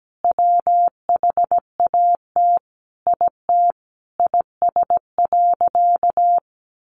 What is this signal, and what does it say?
Morse code, heard as a keyed tone, about 17 wpm, 700 Hz. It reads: WHAT IT IS.
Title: Sentences from Top 700 Words 17wpm